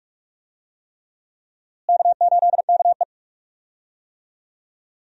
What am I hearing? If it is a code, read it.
Morse code, heard as a keyed tone, about 46 wpm, 700 Hz. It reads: K8KE
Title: USA Call Signs - 4 Characters - Repeated 3x - 45wpm